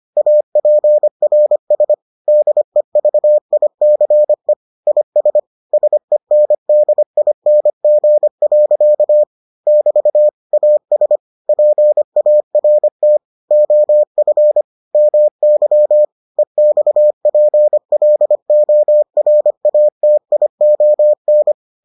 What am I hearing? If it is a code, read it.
Morse code, heard as a keyed tone, about 25 wpm, 600 Hz. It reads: APRS DEVICE IS SENDING. <BT> AS PART OF MY EXPLORATION